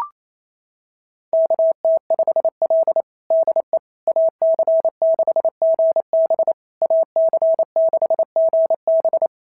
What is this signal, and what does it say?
Morse code, heard as a keyed tone, about 25 wpm, 650 Hz. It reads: KT5L DE AC6GB AC6GB
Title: SOTA QSO Practice 28wpm